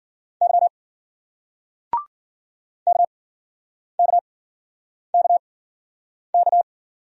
Morse code, heard as a keyed tone, about 54 wpm, 700 Hz. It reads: Y K K K K